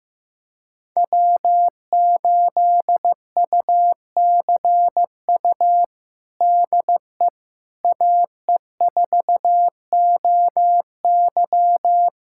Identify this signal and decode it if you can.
Morse code, heard as a keyed tone, about 16 wpm, 700 Hz. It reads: W8UCU DE AE4OY
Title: USA Callsign DE Callsign 15wpm